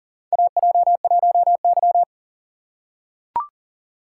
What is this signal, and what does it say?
Morse code, heard as a keyed tone, about 40 wpm, 700 Hz. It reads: AJ1Y
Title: USA Call Signs - 4 Characters - 2x Character Spacing - 40wpm